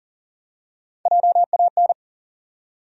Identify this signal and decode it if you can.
Morse code, heard as a keyed tone, about 40 wpm, 700 Hz. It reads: JAN
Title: USA Common Names - Max 3 Letters - 40wpm